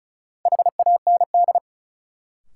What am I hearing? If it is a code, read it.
Morse code, heard as a keyed tone, about 36 wpm, 700 Hz. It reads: HAND
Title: Sets of 3 Words - New Word Drill - 35wpm